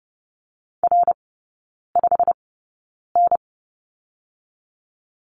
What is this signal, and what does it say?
Morse code, heard as a keyed tone, about 30 wpm, spaced as for 9 wpm, 700 Hz. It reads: R5N
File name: Lesson 15e - Number 5 - Three Characters 30wpm